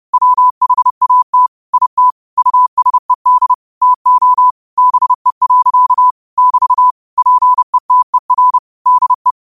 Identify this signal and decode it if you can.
Morse code, heard as a keyed tone, about 30 wpm, 1000 Hz. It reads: WHAT IT USED TO BE. <BT> PETER DE